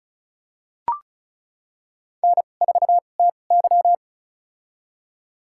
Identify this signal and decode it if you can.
Morse code, heard as a keyed tone, about 36 wpm, spaced as for 25 wpm, 690 Hz. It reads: N4TY
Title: USA Call Signs - 4 Characters - 2x Character Spacing - 35wpm